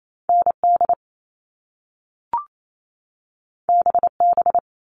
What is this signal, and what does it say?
Morse code, heard as a keyed tone, about 28 wpm, 700 Hz. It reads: ND BB
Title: All Cut Numbers - 2-Digit 28wpm